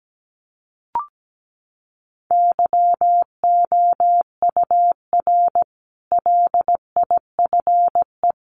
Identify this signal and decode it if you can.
Morse code, heard as a keyed tone, about 17 wpm, 700 Hz. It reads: YOUR LIFE
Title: Sentences from Top 1300 Words 17wpm